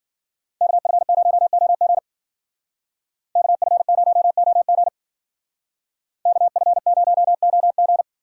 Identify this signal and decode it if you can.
Morse code, heard as a keyed tone, about 52 wpm, 700 Hz. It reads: KF0OG KF0OG KF0OG